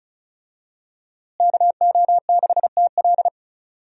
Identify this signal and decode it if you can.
Morse code, heard as a keyed tone, about 36 wpm, 700 Hz. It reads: KO6TL
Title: USA Call Signs - 5 Characters - Repeated 2x - 35wpm